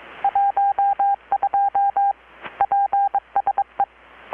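Morse code, heard as a keyed tone, about 22 wpm, 800 Hz. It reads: 12 PSE
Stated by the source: Morse from KPH's Night of Nights XX maritime radiotelegraphy event, received in Chiapas, Mexico about 3,800 km away on a Kenwood R-600 with a grounded 10 m long-wire antenna.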